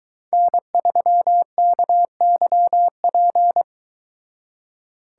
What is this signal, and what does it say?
Morse code, heard as a keyed tone, about 23 wpm, 700 Hz. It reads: N3KYP